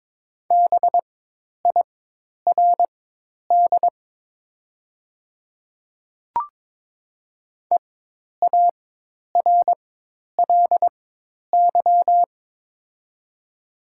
Morse code, heard as a keyed tone, about 22 wpm, spaced as for 10 wpm, 700 Hz. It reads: BIRD EARLY
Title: 4x Character Spacing - Top 500 Words - 22wpm